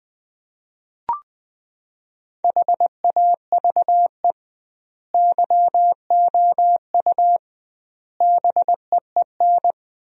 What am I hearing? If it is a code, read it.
Morse code, heard as a keyed tone, about 21 wpm, 700 Hz. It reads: HAVE YOU BEEN
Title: 2x Word-Spacing - Sentences from Top 300 Words - 20wpm